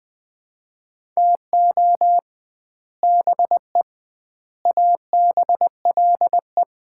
Morse code, heard as a keyed tone, about 20 wpm, 700 Hz. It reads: TO BE ABLE
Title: 2x Word-Spacing - Sets of 5 Words - 20wpm